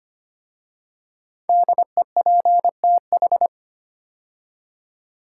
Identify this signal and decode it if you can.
Morse code, heard as a keyed tone, about 25 wpm, 700 Hz. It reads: DEPTH
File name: 5 Letter Words - First 3 Letters Given - 25wpm